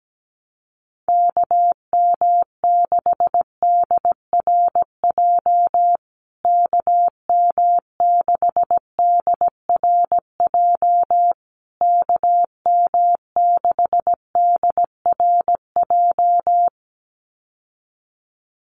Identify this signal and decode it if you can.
Morse code, heard as a keyed tone, about 17 wpm, 700 Hz. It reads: KM6DRJ KM6DRJ KM6DRJ
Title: USA Call Signs Repeated 3x 17wpm